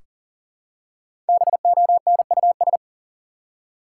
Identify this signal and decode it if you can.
Morse code, heard as a keyed tone, about 40 wpm, 700 Hz. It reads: BONUS
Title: Five Letter Words - 40wpm